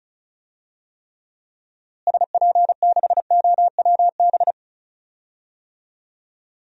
Morse code, heard as a keyed tone, about 34 wpm, 700 Hz. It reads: SP6OWB
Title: DXCC-100 Call Signs - 6 Characters - Repeated 2x - 35wpm